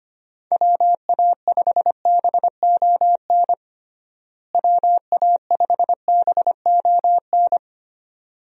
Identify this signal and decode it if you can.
Morse code, heard as a keyed tone, about 25 wpm, 700 Hz. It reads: WA5BON WA5BON
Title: USA Call Signs - 6 Characters - Repeated 2x - 25wpm